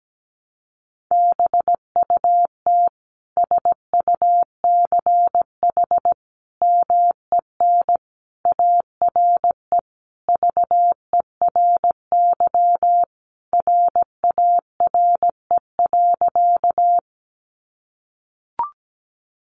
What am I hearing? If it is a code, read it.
Morse code, heard as a keyed tone, about 17 wpm, 700 Hz. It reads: BUT SUCH MEN ARE VERY RARE.
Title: Sentences from Top 1100 Words 17wpm